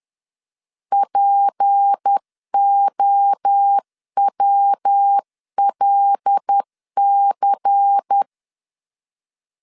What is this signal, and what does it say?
Morse code, heard as a keyed tone, about 10 wpm, 800 Hz. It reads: POWLC
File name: CW / DET / 1 6 1500C S C 10wpm 4252